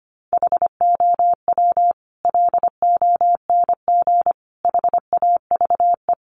Morse code, heard as a keyed tone, about 25 wpm, 700 Hz. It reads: HOW LONG HAVE
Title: Sentences from Top 200 Words 25wpm